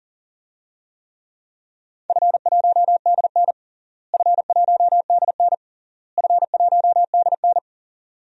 Morse code, heard as a keyed tone, about 40 wpm, 700 Hz. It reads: F1DN F1DN F1DN
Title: DXCC-100 Call Signs - 4 Characters - Repeated 3x - 40wpm